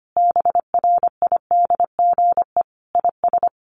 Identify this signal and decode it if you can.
Morse code, heard as a keyed tone, about 25 wpm, 700 Hz. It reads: BRIDGE IS
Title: Sentences from Top 1400 Words 25wpm